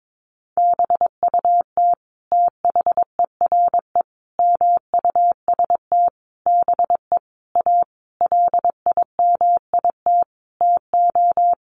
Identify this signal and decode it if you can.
Morse code, heard as a keyed tone, about 22 wpm, 700 Hz. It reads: BUT THERE MUST BE A LIMIT TO
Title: Sentences from Top 1500 Words 22wpm